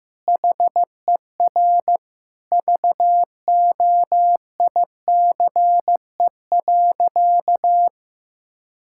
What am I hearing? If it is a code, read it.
Morse code, heard as a keyed tone, about 15 wpm, 700 Hz. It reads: HER VOICE.
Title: Sentences from Top 1600 Words 15wpm